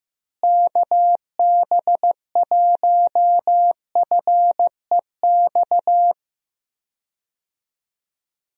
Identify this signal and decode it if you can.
Morse code, heard as a keyed tone, about 15 wpm, 700 Hz. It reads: KB1FEX
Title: USA Call Signs - 6 Characters - Repeated 2x - 15wpm